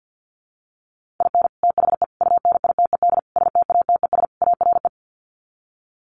Morse code, heard as a keyed tone, about 49 wpm, 700 Hz. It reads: IN THE UNITED STATES ARE
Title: Sets of 5 Words Encore 50wpm